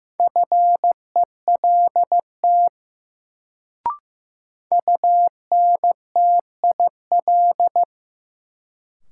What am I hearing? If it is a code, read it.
Morse code, heard as a keyed tone, about 15 wpm, 700 Hz. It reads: FELT UNTIL
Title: Sets of 5 Words Encore - New Word Drill - 15wpm